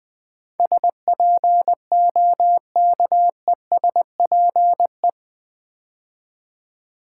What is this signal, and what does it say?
Morse code, heard as a keyed tone, about 20 wpm, 700 Hz. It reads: SPOKESPE
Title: Build A Word - 4 Letters At A Time - 12-Letter Words - 20wpm